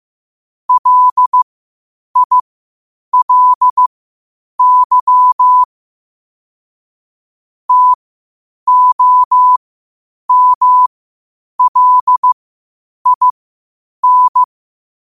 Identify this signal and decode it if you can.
Morse code, heard as a keyed tone, about 15 wpm, spaced as for 8 wpm, 1000 Hz. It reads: LILY TOMLIN
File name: qotdfiles / 8wpm